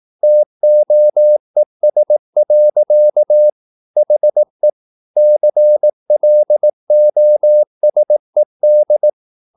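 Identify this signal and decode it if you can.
Morse code, heard as a keyed tone, about 18 wpm, 600 Hz. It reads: TOES. HE CLOSED
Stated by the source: Morse practice at 18 wpm, English text, from CW med Gnister.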